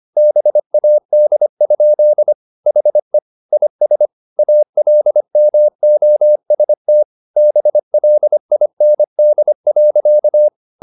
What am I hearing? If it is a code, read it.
Morse code, heard as a keyed tone, about 25 wpm, 600 Hz. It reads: BAD? HE IS ALMOST BLIND.
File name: the old man and the sea part 1 25wpm